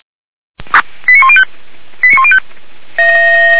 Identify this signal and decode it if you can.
Five-tone selective call (ZVEI), heard as quick runs of five tones, tones of about 70 ms. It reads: 89105 89105